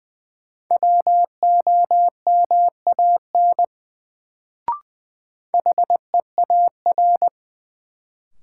Morse code, heard as a keyed tone, about 20 wpm, 700 Hz. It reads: WOMAN HEAR
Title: Sets of 5 Words - New Word Drill - 20wpm